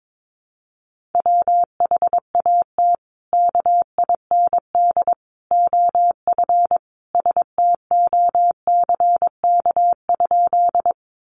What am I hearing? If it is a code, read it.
Morse code, heard as a keyed tone, about 22 wpm, 700 Hz. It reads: WHAT KIND OF STOCK?
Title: Sentences from Top 800 Words 22wpm